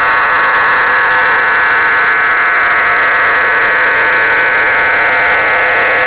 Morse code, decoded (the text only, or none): none